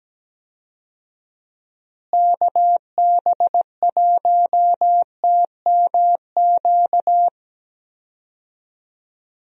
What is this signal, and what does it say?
Morse code, heard as a keyed tone, about 17 wpm, 700 Hz. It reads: KB1TMQ
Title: USA Call Signs - 6 Characters - Repeated 2x - 17wpm